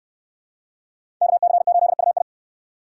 Morse code, heard as a keyed tone, about 69 wpm, 700 Hz. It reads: KC8UI